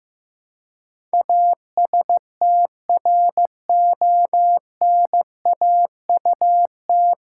Morse code, heard as a keyed tone, about 15 wpm, 700 Hz. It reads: ASTRONAUT
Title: Pangram Challenge v2 - 15wpm